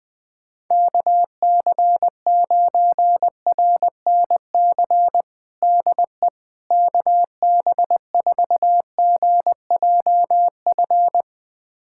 Morse code, heard as a keyed tone, about 20 wpm, 700 Hz. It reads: KC9RNC DE KB4GJF
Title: USA Callsign DE Callsign 20wpm